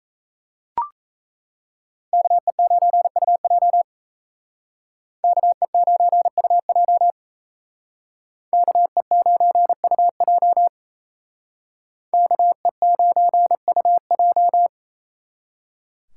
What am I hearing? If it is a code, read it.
Morse code, heard as a keyed tone, about 34 wpm, 700 Hz. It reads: KE9UJ KE9UJ KE9UJ KE9UJ
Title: Speed Racing - USA 2x2 Call Signs 28wpm